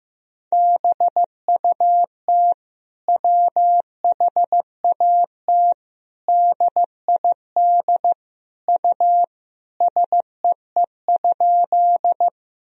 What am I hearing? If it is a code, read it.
Morse code, heard as a keyed tone, about 16 wpm, 700 Hz. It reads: BUT WHAT DID U SEE?
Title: Sentences from Top 100 Words and Abbreviations 15wpm